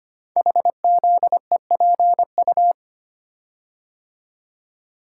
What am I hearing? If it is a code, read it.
Morse code, heard as a keyed tone, about 25 wpm, 700 Hz. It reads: HZEPU
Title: Code Groups - 5 Letters - 25wpm